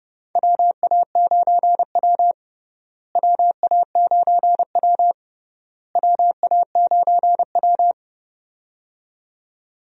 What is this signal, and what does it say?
Morse code, heard as a keyed tone, about 31 wpm, 700 Hz. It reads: WA9W WA9W WA9W
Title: USA Call Signs - 4 Characters - Repeated 3x - 30wpm